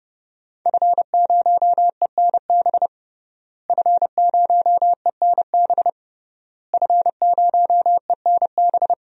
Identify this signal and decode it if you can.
Morse code, heard as a keyed tone, about 30 wpm, 700 Hz. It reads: F0ENB F0ENB F0ENB
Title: DXCC-100 Call Signs - 5 Characters - Repeated 3x - 30wpm